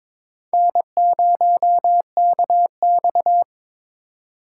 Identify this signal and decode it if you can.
Morse code, heard as a keyed tone, about 26 wpm, 700 Hz. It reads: N0KX
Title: Speed Racing - USA 1x2 Call Signs 22wpm